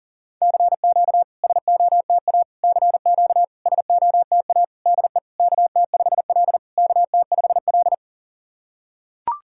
Morse code, heard as a keyed tone, about 40 wpm, 690 Hz. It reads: CQ SOTA CQ SOTA DE KT5L KT5L
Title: SOTA QSO Practice 40wpm